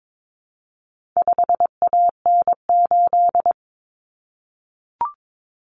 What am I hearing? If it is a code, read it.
Morse code, heard as a keyed tone, about 22 wpm, 700 Hz. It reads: 5AN8
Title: Common Cut Numbers - 4-Digit 22wpm